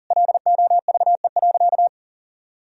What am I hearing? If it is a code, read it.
Morse code, heard as a keyed tone, about 40 wpm, 700 Hz. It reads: LOVE.
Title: Sentences from Top 1000 Words - Review Part 3 - 40wpm